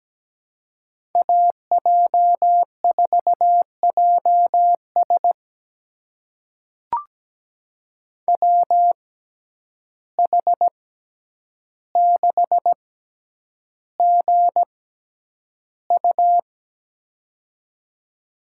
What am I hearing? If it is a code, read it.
Morse code, heard as a keyed tone, about 17 wpm, spaced as for 9 wpm, 700 Hz. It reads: AJ4JS WH6GU
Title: USA Call Signs - 5 Characters - 6x Character Spacing - 17wpm